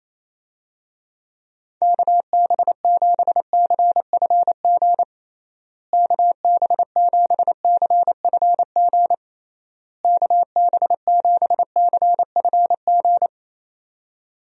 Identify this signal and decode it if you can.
Morse code, heard as a keyed tone, about 28 wpm, 700 Hz. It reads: KB7CFG KB7CFG KB7CFG
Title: USA Call Signs - 6 Characters - Repeated 3x - 28wpm